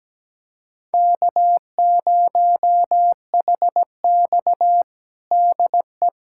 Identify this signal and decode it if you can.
Morse code, heard as a keyed tone, about 17 wpm, 700 Hz. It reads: K0HX DE
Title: USA Callsign DE Callsign 17wpm